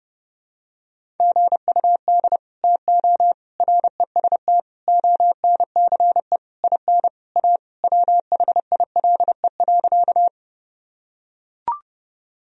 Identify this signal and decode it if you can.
Morse code, heard as a keyed tone, about 30 wpm, 700 Hz. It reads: GUD TO REST ONCE IN A WHILE.
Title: Sentences from Top 500 Words and Abbreviations 30wpm